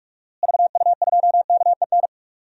Morse code, heard as a keyed tone, about 42 wpm, 700 Hz. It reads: VU2KEN
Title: DXCC-100 Call Signs - 6 Characters - Repeated 3x - 45wpm